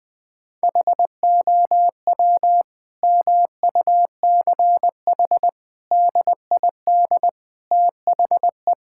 Morse code, heard as a keyed tone, about 21 wpm, 700 Hz. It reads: HOW MUCH DID THE
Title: Sentences from Top 1000 Words - Review Part 6 - 20wpm